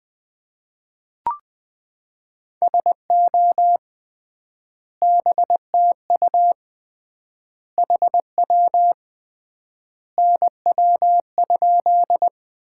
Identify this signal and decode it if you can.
Morse code, heard as a keyed tone, about 20 wpm, 700 Hz. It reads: SO BTU HW NW?
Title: Common QSO Phrases - 3x Word-Spacing - 20wpm